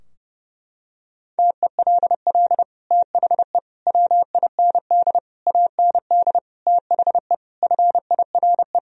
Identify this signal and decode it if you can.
Morse code, heard as a keyed tone, about 30 wpm, 700 Hz. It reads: TELL THE WIND AND THE FIRE
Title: Sentences from Top 1000 Words - Review Part 2 - 30wpm